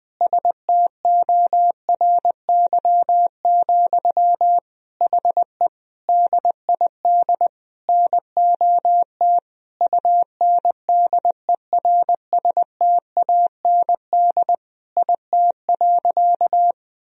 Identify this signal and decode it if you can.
Morse code, heard as a keyed tone, about 20 wpm, 700 Hz. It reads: STORY, HE DID NOT UNDERSTAND IT.
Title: Sentences from Top 1000 Words - Review Part 3 - 20wpm